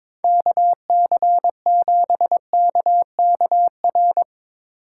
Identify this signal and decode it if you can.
Morse code, heard as a keyed tone, about 24 wpm, 700 Hz. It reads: KC7KKR
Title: Speed Racing - USA 2x3 Call Signs 22wpm